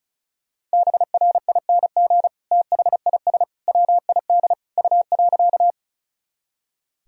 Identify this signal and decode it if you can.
Morse code, heard as a keyed tone, about 34 wpm, 700 Hz. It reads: BRING THIS WID U.
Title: Sentences from Top 500 Words and Abbreviations 35wpm